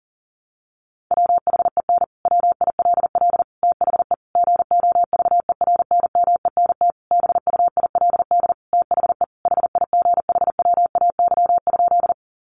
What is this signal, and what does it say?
Morse code, heard as a keyed tone, about 40 wpm, 700 Hz. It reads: WHEN WILL THE GOVERNMENT BUILD THE HIGHWAY?
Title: Sentences from Top 1400 Words 40wpm